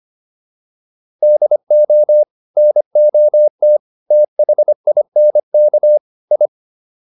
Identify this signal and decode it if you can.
Morse code, heard as a keyed tone, about 25 wpm, 600 Hz. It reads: DO NOT THINK I